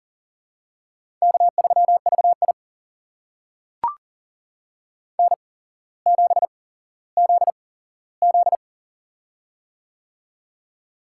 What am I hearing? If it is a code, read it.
Morse code, heard as a keyed tone, about 40 wpm, spaced as for 17 wpm, 700 Hz. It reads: K3VI N7ZZ
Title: USA Call Signs - 4 Characters - 8x Character Spacing - 40wpm